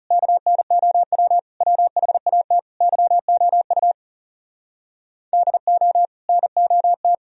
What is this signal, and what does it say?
Morse code, heard as a keyed tone, about 37 wpm, 700 Hz. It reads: KNOW WHAT YOU DO NOT
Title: Speed-Racing - Sets of 5 Words 30wpm